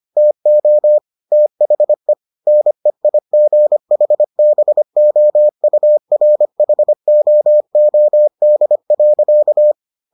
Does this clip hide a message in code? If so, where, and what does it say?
Yes: Morse code from 0.2 to 9.7 s, heard as a keyed tone, about 25 wpm, 600 Hz. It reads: TO THE NEIGHBOURHOOD.